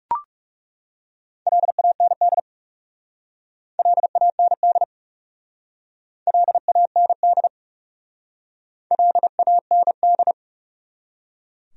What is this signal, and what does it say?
Morse code, heard as a keyed tone, about 36 wpm, 700 Hz. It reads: LAND LAND LAND LAND
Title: Speed-Racing Top 500 Words 30wpm